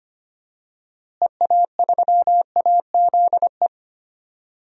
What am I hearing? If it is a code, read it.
Morse code, heard as a keyed tone, about 25 wpm, 700 Hz. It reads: EA3AZE
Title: DXCC-100 Call Signs - 6 Characters - Repeated 3x - 25wpm